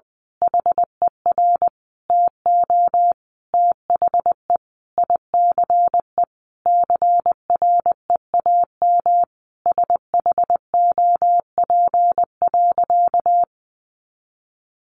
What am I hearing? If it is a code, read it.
Morse code, heard as a keyed tone, about 20 wpm, 700 Hz. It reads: HER TO THE ICE CREAM SHOP.
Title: Additional Easy Phrases 20wpm